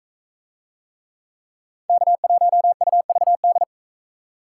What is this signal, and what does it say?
Morse code, heard as a keyed tone, about 37 wpm, 700 Hz. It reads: K1UVD